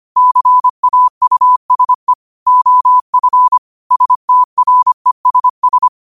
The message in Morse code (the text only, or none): CAUSE OF STRESS